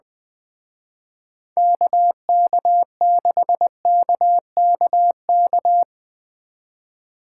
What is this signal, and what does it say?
Morse code, heard as a keyed tone, about 23 wpm, 700 Hz. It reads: KK6KKK
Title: Speed Racing - USA 2x3 Call Signs 20wpm